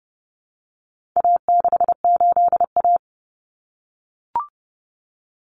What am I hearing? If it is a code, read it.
Morse code, heard as a keyed tone, about 30 wpm, 700 Hz. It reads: A68A
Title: Common Cut Numbers - 4-Digit 30wpm